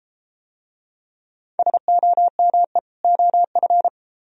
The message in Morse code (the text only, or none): SOME OF